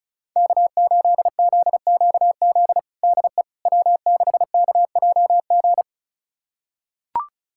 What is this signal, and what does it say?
Morse code, heard as a keyed tone, about 36 wpm, 700 Hz. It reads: K8ZQZ DE W6KJG
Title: USA Callsign DE Callsign - 5 Characters - 35wpm